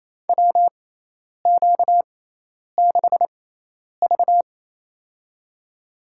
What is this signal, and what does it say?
Morse code, heard as a keyed tone, about 28 wpm, spaced as for 10 wpm, 700 Hz. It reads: WQ6V
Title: USA Call Signs - 4 Characters - 6x Character Spacing - 28wpm